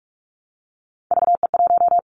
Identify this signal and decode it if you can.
Morse code, heard as a keyed tone, about 44 wpm, 700 Hz. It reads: VE1
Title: Callsign Prefixes for Top 25 DXCC - 45wpm